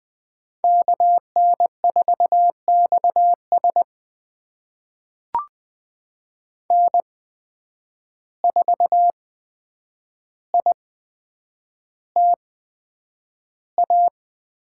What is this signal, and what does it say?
Morse code, heard as a keyed tone, about 20 wpm, spaced as for 9 wpm, 700 Hz. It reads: KN4XS N4ITA
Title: USA Call Signs - 5 Characters - 8x Character Spacing - 20wpm